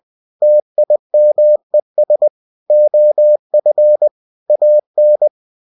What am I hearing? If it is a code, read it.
Morse code, heard as a keyed tone, about 20 wpm, 600 Hz. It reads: TIMES OF AN
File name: cw-books / winnie-the-pooh / winnie-20wpm-page-0062